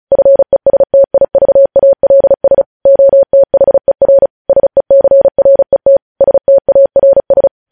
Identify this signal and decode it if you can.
Morse code, heard as a keyed tone, about 36 wpm, 550 Hz. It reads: FESTIVALS OTHER SECRET STARS
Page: news headlines in morse code at 35 wpm